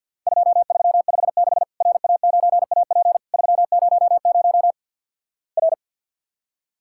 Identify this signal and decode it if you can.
Morse code, heard as a keyed tone, about 48 wpm, 700 Hz. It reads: 2356 RA9AP 300 R
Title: Worked All Europe DX Contest - QTC Exchange with Repeats - 50wpm